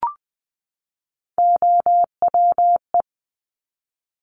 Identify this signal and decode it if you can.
Morse code, heard as a keyed tone, about 20 wpm, 700 Hz. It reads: OWE
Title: Top Three Letter Combinations in Words 20wpm